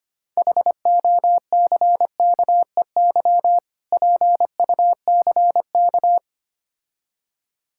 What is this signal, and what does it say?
Morse code, heard as a keyed tone, about 25 wpm, 700 Hz. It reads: HOCKEY PUCK
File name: practice / 25wpm / Difficult Words with Hints - 25wpm